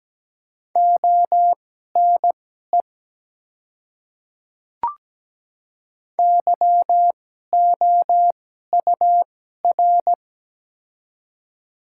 Morse code, heard as a keyed tone, about 17 wpm, spaced as for 12 wpm, 700 Hz. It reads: ONE YOUR
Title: 2x Character Spacing - Top 100 Words - 17wpm